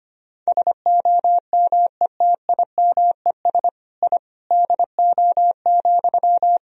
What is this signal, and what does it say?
Morse code, heard as a keyed tone, about 25 wpm, 700 Hz. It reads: SOMETIMES I DO,
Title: Sentences from Top 1000 Words - Review Part 3 - 25wpm